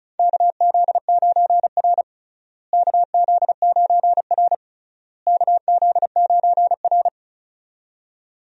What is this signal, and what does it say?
Morse code, heard as a keyed tone, about 36 wpm, 700 Hz. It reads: KZ9R KZ9R KZ9R
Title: USA Call Signs - 4 Characters - Repeated 3x - 35wpm